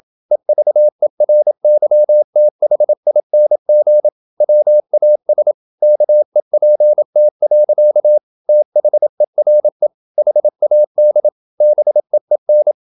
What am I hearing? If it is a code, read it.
Morse code, heard as a keyed tone, about 27 wpm, 600 Hz. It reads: EVERYTHING WAS KEPT. THERE HAD BEEN
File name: Never 0027wpm